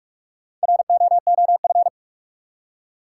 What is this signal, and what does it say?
Morse code, heard as a keyed tone, about 47 wpm, 700 Hz. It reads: ROOF
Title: Four Letter Words - 45wpm